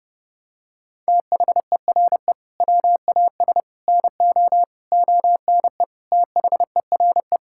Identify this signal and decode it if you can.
Morse code, heard as a keyed tone, about 30 wpm, 700 Hz. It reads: THERE WAS NO ONE THERE
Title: Sentences from Top 1600 Words 30wpm